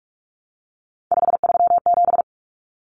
Morse code, heard as a keyed tone, about 44 wpm, 700 Hz. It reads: 537
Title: Common Cut Numbers - 3-Digit 45wpm